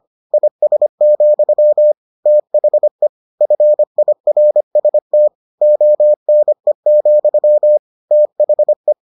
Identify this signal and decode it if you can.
Morse code, heard as a keyed tone, about 25 wpm, 600 Hz. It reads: IS, THE FIRST ONE, THE